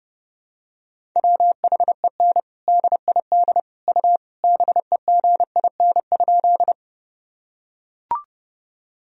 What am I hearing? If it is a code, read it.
Morse code, heard as a keyed tone, about 30 wpm, 700 Hz. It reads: WHEN DID U BEGIN?
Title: Sentences from Top 300 Words and Abbreviations 30wpm